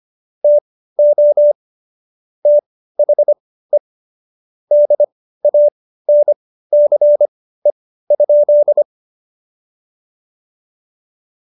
Morse code, heard as a keyed tone, about 25 wpm, spaced as for 15 wpm, 590 Hz. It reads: TO THE DANCE?
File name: Jokes-25-char-15-eff